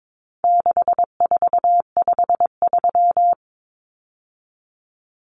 Morse code, heard as a keyed tone, about 22 wpm, 700 Hz. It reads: 6453
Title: Numbers - 4-Digit 22wpm